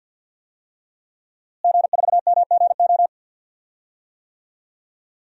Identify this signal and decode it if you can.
Morse code, heard as a keyed tone, about 52 wpm, 700 Hz. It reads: G4MGO
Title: DXCC-100 Call Signs - 5 Characters - Repeated 2x - 50wpm